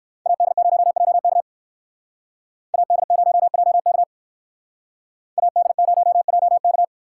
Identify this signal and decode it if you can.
Morse code, heard as a keyed tone, about 60 wpm, 700 Hz. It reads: AD0JK AD0JK AD0JK